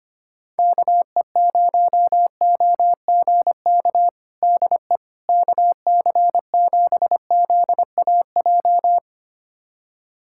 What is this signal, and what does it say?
Morse code, heard as a keyed tone, about 25 wpm, 700 Hz. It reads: KE0OGK DE KC7ZAJ
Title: USA Callsign DE Callsign - 6 Characters - 25wpm